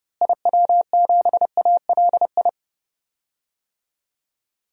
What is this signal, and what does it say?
Morse code, heard as a keyed tone, about 30 wpm, 700 Hz. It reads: IW7ALI
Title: DXCC-100 Call Signs - 6 Characters - Repeated 3x - 30wpm